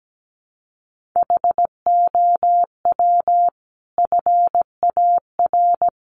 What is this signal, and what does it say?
Morse code, heard as a keyed tone, about 17 wpm, 700 Hz. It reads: HOW FAR
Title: Sentences from Top 1500 Words 17wpm